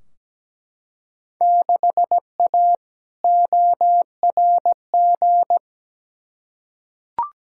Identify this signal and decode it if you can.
Morse code, heard as a keyed tone, about 17 wpm, 700 Hz. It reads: 6A ORG
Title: Field Day Exchanges Encore - 17wpm